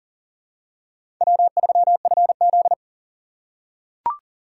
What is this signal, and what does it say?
Morse code, heard as a keyed tone, about 39 wpm, 700 Hz. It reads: W3FZ
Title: USA Call Signs - 4x Character Spacing - 40wpm